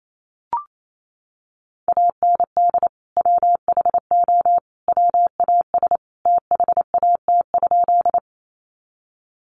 Morse code, heard as a keyed tone, about 28 wpm, 700 Hz. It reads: AND WHO WAS THAT?
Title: Sentences from Top 100 Words 28wpm